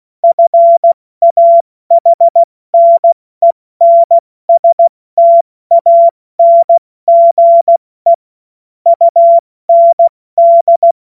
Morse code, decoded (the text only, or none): FAHNENSTANGE UND